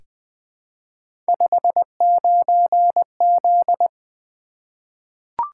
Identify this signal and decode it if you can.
Morse code, heard as a keyed tone, about 20 wpm, 700 Hz. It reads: 59Z
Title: Three Letter-Number Combinations - 20wpm